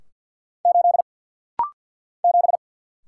Morse code, heard as a keyed tone, about 55 wpm, 700 Hz. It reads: 8 7
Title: Single Letter-Number Alt - Rapid-Fire 50wpm